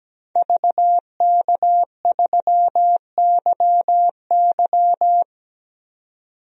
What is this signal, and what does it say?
Morse code, heard as a keyed tone, about 17 wpm, 700 Hz. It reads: VK3YY
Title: DXCC-100 Call Signs - 5 Characters - 17wpm